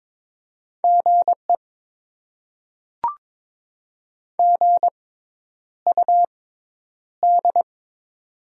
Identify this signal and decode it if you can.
Morse code, heard as a keyed tone, about 21 wpm, spaced as for 10 wpm, 700 Hz. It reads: GE GUD
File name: Top 20 QSO Elements - 6x Character Spacing - 22wpm